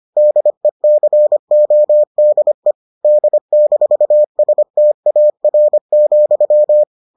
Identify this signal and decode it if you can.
Morse code, heard as a keyed tone, about 25 wpm, 600 Hz. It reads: DECODE D-STAR,